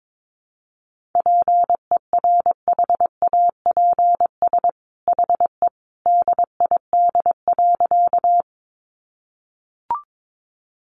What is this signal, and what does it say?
Morse code, heard as a keyed tone, about 22 wpm, 700 Hz. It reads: PERHAPS HE DID.
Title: Sentences from Top 500 Words 22wpm